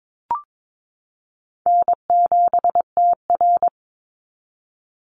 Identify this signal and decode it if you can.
Morse code, heard as a keyed tone, about 22 wpm, 700 Hz. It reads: N7TR
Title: Call Signs Repeated 2x - 22wpm